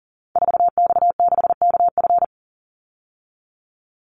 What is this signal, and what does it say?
Morse code, heard as a keyed tone, about 40 wpm, 700 Hz. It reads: 4X6KF
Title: Call Signs Repeated 3x - 40wpm